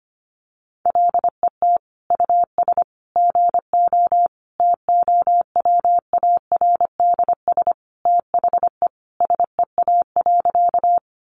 Morse code, heard as a keyed tone, about 25 wpm, 700 Hz. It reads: LET US GO TOWARDS THE SEA.
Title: Sentences from Top 1100 Words 25wpm